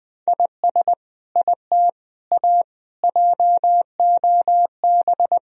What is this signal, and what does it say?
Morse code, heard as a keyed tone, about 21 wpm, 700 Hz. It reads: IS IT A JOB
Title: Sentences from Top 1000 Words - Review Part 3 - 20wpm